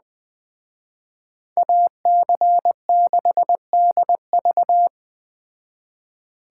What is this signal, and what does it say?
Morse code, heard as a keyed tone, about 20 wpm, 700 Hz. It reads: AC6DV
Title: Speed Racing - USA 2x2 Call Signs 17wpm